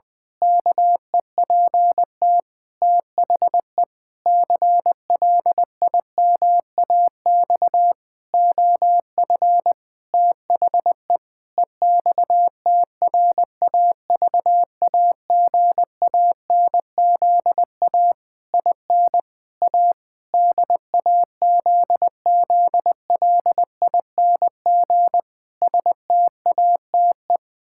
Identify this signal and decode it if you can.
Morse code, heard as a keyed tone, about 20 wpm, 700 Hz. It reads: KEPT THE CLIMAX OF THE EXTRAVAGANZA IN A DAZZLING STATE
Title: Pangram Challenge v2 - 20wpm